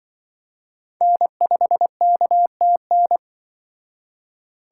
Morse code, heard as a keyed tone, about 25 wpm, 700 Hz. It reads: N5KTN
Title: Speed Racing - USA 1x3 Call Signs 20wpm